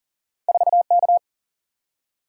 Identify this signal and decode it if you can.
Morse code, heard as a keyed tone, about 42 wpm, 700 Hz. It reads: 4K
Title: Single-Letter Single-Number 40wpm